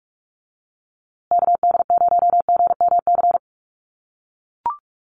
Morse code, heard as a keyed tone, about 44 wpm, 700 Hz. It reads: KD0GMC
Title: USA Call Signs Repeated 3x 45wpm